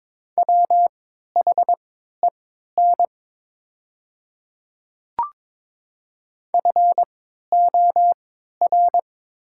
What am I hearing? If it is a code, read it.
Morse code, heard as a keyed tone, about 22 wpm, spaced as for 12 wpm, 700 Hz. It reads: WHEN FOR
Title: 3x Character Spacing - Top 100 Words - 22wpm